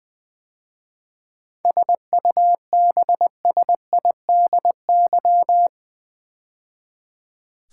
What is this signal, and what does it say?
Morse code, heard as a keyed tone, about 20 wpm, 700 Hz. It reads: SUBSIDY
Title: 7 Letter Words - First 3 Letters Given - 20wpm